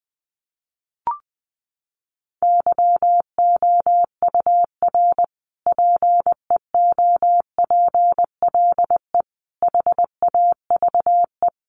Morse code, heard as a keyed tone, about 20 wpm, 700 Hz. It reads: YOUR PEOPLE HAVE
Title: Sentences from Top 1400 Words 20wpm